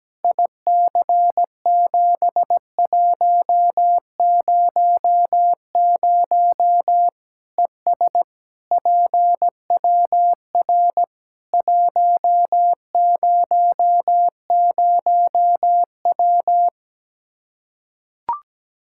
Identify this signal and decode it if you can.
Morse code, heard as a keyed tone, about 17 wpm, 700 Hz. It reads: IC7100 ES PWR 100W
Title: K1USN SST QSO Practice 17wpm